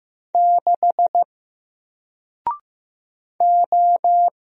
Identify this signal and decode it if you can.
Morse code, heard as a keyed tone, about 15 wpm, 700 Hz. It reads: 6 O
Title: Single Character (Letter:Number:Punctuation:Prosign) Alt 15wpm